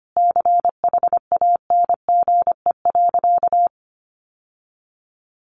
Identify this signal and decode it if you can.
Morse code, heard as a keyed tone, about 25 wpm, 700 Hz. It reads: CHANGE.
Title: Sentences from Top 1500 Words 25wpm